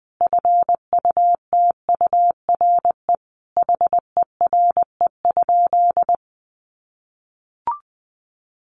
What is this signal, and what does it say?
Morse code, heard as a keyed tone, about 19 wpm, 700 Hz. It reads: FUTURE HERE?
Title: Sentences from Top 1500 Words 20wpm